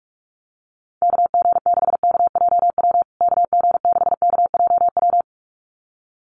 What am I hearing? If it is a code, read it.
Morse code, heard as a keyed tone, about 44 wpm, 700 Hz. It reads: KG6KJW KG6KJW
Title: USA Call Signs Repeated 2x 45wpm